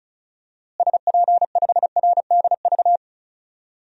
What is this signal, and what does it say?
Morse code, heard as a keyed tone, about 34 wpm, 700 Hz. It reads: SP5RDV
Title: DXCC-100 Call Signs - 6 Characters - Repeated 3x - 35wpm